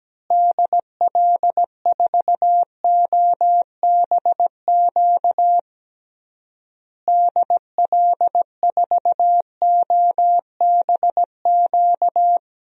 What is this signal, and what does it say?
Morse code, heard as a keyed tone, about 17 wpm, 700 Hz. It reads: DL4OBQ DL4OBQ
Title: USA Call Signs - 6 Characters - Repeated 3x - 17wpm